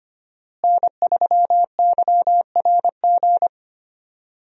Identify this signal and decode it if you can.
Morse code, heard as a keyed tone, about 25 wpm, 700 Hz. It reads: N3YRG
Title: USA Call Signs - 5 Characters - Repeated 3x - 25wpm